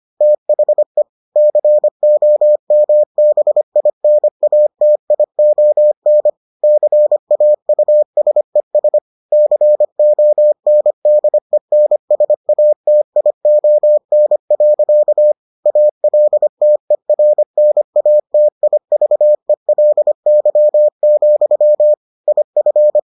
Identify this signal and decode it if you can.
Morse code, heard as a keyed tone, about 25 wpm, 600 Hz. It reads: THE COMBINATION CAUSES CONDENSATION. ALTERNATIVELY, IF